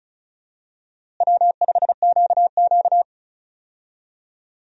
Morse code, heard as a keyed tone, about 34 wpm, 700 Hz. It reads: W5QQ
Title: USA Call Signs - 4 Characters - Repeated 3x - 35wpm